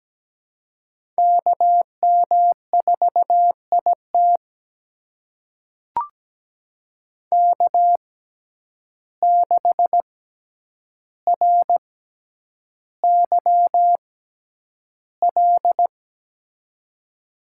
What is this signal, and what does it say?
Morse code, heard as a keyed tone, about 17 wpm, spaced as for 9 wpm, 700 Hz. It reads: KM4IT K6RYL
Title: USA Call Signs - 5 Characters - 6x Character Spacing - 17wpm